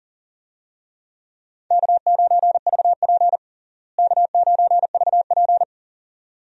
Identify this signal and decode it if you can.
Morse code, heard as a keyed tone, about 40 wpm, 700 Hz. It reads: K9VP K9VP
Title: USA Call Signs - 4 Characters - Repeated 2x - 40wpm